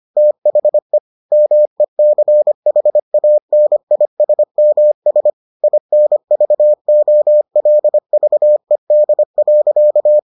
Morse code, heard as a keyed tone, about 25 wpm, 600 Hz. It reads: THE MECHANISMS INVOLVED.